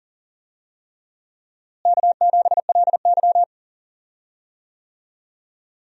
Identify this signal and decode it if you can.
Morse code, heard as a keyed tone, about 40 wpm, 700 Hz. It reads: K7LY
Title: USA Call Signs - 4 Characters - Repeated 3x - 40wpm